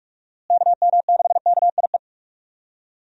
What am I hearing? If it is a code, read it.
Morse code, heard as a keyed tone, about 46 wpm, 700 Hz. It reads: KM6KIE
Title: USA Call Signs - 6 Characters - Repeated 3x - 45wpm